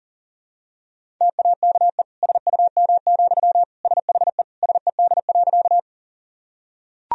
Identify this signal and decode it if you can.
Morse code, heard as a keyed tone, about 40 wpm, 700 Hz. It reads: TAKE SUM, SHE SED.
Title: Sentences from Top 200 Words and Abbreviations 40wpm